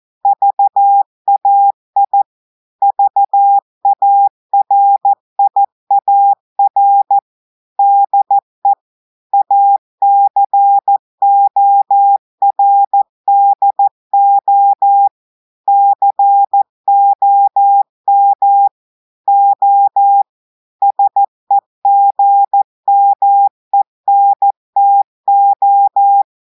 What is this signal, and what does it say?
Morse code, heard as a keyed tone, about 14 wpm, 800 Hz. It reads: VAI VARIAR DE ACORDO COM O SEGMENTO
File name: audios / portugais / portugais texte en clair 2 vitesse 14 mots minutes